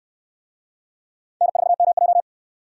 Morse code, heard as a keyed tone, about 61 wpm, 700 Hz. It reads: N4MJ